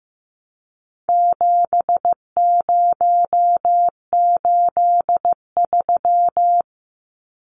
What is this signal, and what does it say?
Morse code, heard as a keyed tone, about 15 wpm, 700 Hz. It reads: 7083
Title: Numbers - 4-Digit 15wpm